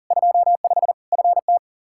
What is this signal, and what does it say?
Morse code, heard as a keyed tone, about 40 wpm, 700 Hz. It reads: 25 FT
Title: Practice QSO - Exchange 2 - Info - 40wpm